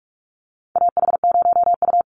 Morse code, heard as a keyed tone, about 44 wpm, 700 Hz. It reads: AH0U
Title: Call Signs - 45wpm